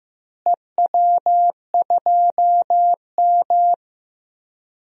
Morse code, heard as a keyed tone, about 15 wpm, 700 Hz. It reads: EW2M
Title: DXCC-100 Call Signs - 4 Characters - Repeated 2x - 15wpm